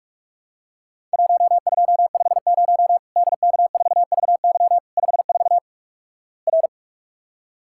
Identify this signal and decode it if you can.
Morse code, heard as a keyed tone, about 48 wpm, 700 Hz. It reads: 1250 DK4VY 54 R
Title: Worked All Europe DX Contest - QTC Exchange - 45wpm